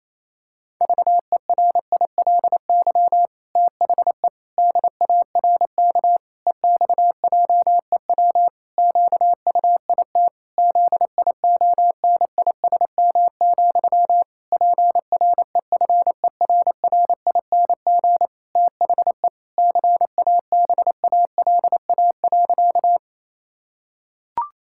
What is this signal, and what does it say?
Morse code, heard as a keyed tone, about 28 wpm, 700 Hz. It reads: VERILY THE DARK EXJEW QUIT ZIONISM, PREFERRING THE CABALA.
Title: Pangram Challenge v2 - 28wpm